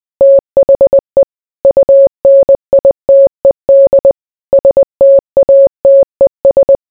Morse code, heard as a keyed tone, about 20 wpm, 550 Hz. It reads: THE UNITED STATES